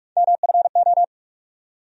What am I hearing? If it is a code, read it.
Morse code, heard as a keyed tone, about 46 wpm, 700 Hz. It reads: MFO
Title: USA Call Sign Postfixes - 3 Letters 45wpm